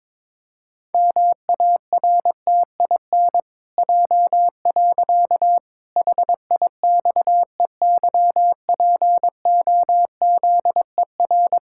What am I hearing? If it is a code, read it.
Morse code, heard as a keyed tone, about 22 wpm, 700 Hz. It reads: MARTIN J. HIXEYPOZER
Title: Pangram Challenge v2 - 22wpm